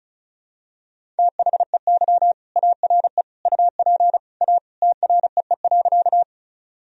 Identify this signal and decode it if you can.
Morse code, heard as a keyed tone, about 36 wpm, 700 Hz. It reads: THEY ARE UP A TREE.
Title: Sentences from Top 1000 Words - Review Part 1 - 35wpm